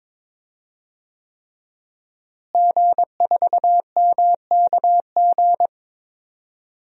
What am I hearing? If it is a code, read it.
Morse code, heard as a keyed tone, about 22 wpm, 700 Hz. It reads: G4MKG
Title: DXCC-100 Call Signs - 5 Characters - Repeated 3x - 22wpm